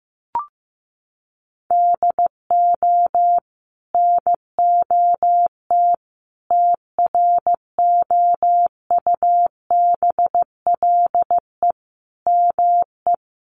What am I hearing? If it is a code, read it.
Morse code, heard as a keyed tone, about 15 wpm, 700 Hz. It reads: DO NOT TROUBLE ME
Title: Sentences from Top 1200 Words 15wpm